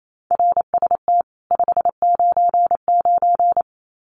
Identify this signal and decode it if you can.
Morse code, heard as a keyed tone, about 28 wpm, 700 Hz. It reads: RST 599
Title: Common QSO Phrases 28wpm